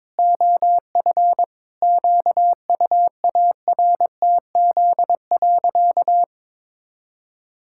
Morse code, heard as a keyed tone, about 22 wpm, 700 Hz. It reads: OF QUARTZ.
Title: Pangram Challenge v2 - 22wpm